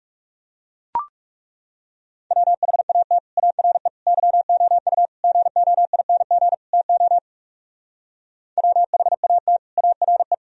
Morse code, heard as a keyed tone, about 38 wpm, 700 Hz. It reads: WHAT ARE YOU GOING TO WHAT ARE